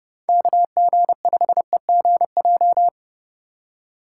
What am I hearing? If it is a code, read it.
Morse code, heard as a keyed tone, about 31 wpm, 700 Hz. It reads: KG5EGJ
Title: USA Call Signs - 4x Character Spacing - 30wpm